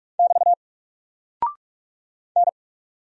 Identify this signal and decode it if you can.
Morse code, heard as a keyed tone, about 46 wpm, 700 Hz. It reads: <BT> N
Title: Single Character (Letter:Number:Punctuation:Prosign) 45wpm